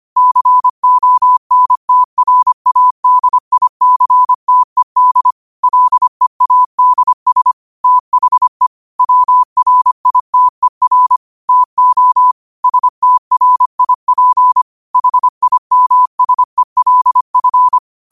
Morse code, heard as a keyed tone, about 25 wpm, 1000 Hz. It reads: CONTRADICTED LEADS THE WRITER TO STRIP HIMSELF
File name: qotdfiles / 25wpm